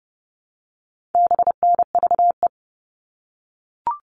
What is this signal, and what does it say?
Morse code, heard as a keyed tone, about 30 wpm, 700 Hz. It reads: BNVE
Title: All Cut Numbers - 4-Digit 30wpm